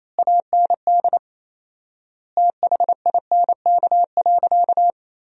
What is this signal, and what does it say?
Morse code, heard as a keyed tone, about 28 wpm, 700 Hz. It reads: AND THINK.
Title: 4x Word-Spacing - Sentences from Top 500 Words - 28wpm